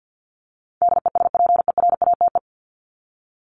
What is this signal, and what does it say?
Morse code, heard as a keyed tone, about 48 wpm, 700 Hz. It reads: DESPERATE
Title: Top 1751-2000 Words 50wpm